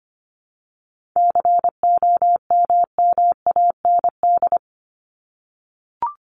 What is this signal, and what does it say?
Morse code, heard as a keyed tone, about 25 wpm, 700 Hz. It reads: COMMAND
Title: Top 1501-1750 Words 25wpm